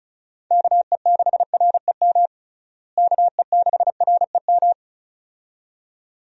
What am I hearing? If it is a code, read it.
Morse code, heard as a keyed tone, about 34 wpm, 700 Hz. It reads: KE6REM KE6REM
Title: USA Call Signs - 6 Characters - Repeated 2x - 35wpm